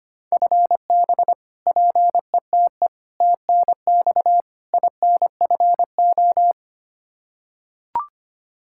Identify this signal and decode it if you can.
Morse code, heard as a keyed tone, about 25 wpm, 700 Hz. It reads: FB PETE TNX INFO
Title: Practice QSO - Exchange 3 - More Info - 25wpm